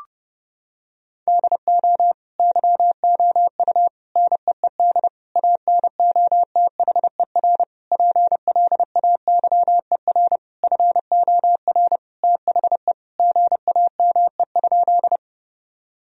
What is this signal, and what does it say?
Morse code, heard as a keyed tone, about 30 wpm, 700 Hz. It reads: DO YOU NEED ANOTHER PLAYER FOR THE GAME?
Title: Sentences from Top 1000 Words - Review Part 4 - 30wpm